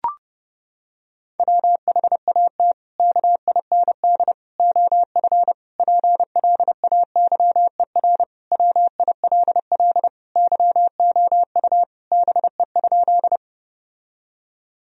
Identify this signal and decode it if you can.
Morse code, heard as a keyed tone, about 32 wpm, 700 Hz. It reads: WHAT KIND OF PLAYER WILL YOU BE?
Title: Sentences from Top 1000 Words - Review Part 4 - 30wpm